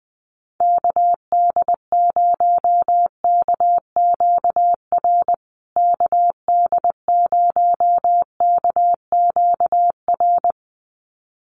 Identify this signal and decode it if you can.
Morse code, heard as a keyed tone, about 20 wpm, 700 Hz. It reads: KD0KQR KD0KQR
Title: USA Call Signs Repeated 2x 20wpm